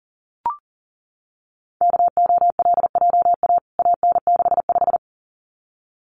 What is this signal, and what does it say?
Morse code, heard as a keyed tone, about 40 wpm, 700 Hz. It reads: KOLJA AN65
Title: CWT Exchange 40wpm